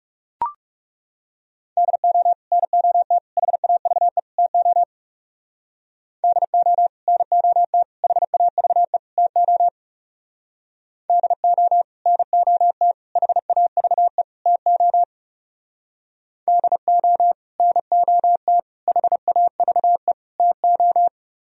Morse code, heard as a keyed tone, about 38 wpm, 700 Hz. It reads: DO NOT HAVE TO DO NOT HAVE TO DO NOT HAVE TO DO NOT HAVE TO